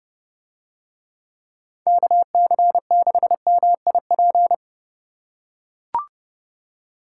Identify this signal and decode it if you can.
Morse code, heard as a keyed tone, about 28 wpm, 700 Hz. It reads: KC6MIP